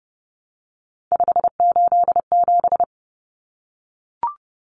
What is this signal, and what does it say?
Morse code, heard as a keyed tone, about 30 wpm, 700 Hz. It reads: 587
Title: Numbers - 3-Digit 30wpm